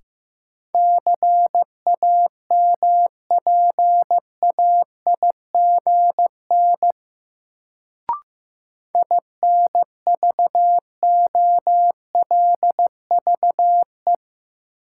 Morse code, heard as a keyed tone, about 15 wpm, 700 Hz. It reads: CAMPAIGN INVOLVE
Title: Sentences from Top 1200 Words - New Word Drill Alt - 15wpm